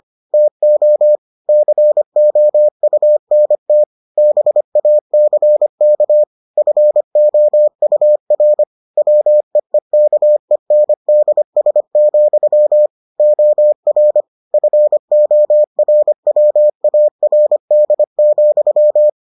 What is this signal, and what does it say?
Morse code, heard as a keyed tone, about 25 wpm, 600 Hz. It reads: TO COUNT BACK FOUR WEEKENDS, OR FORWARD,